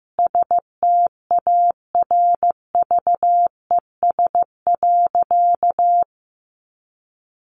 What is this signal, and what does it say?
Morse code, heard as a keyed tone, about 15 wpm, 700 Hz. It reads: STARVES.
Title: A Sound Of Thunder 15wpm